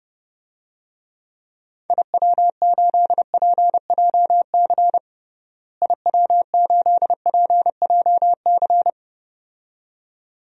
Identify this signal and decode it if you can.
Morse code, heard as a keyed tone, about 30 wpm, 700 Hz. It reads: IW8PJC IW8PJC
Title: DXCC-100 Call Signs - 6 Characters - Repeated 2x - 30wpm